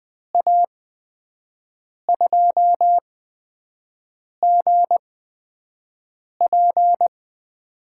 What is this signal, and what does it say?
Morse code, heard as a keyed tone, about 20 wpm, spaced as for 5 wpm, 700 Hz. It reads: A2GP
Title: USA Call Signs - 8x Character Spacing - 20wpm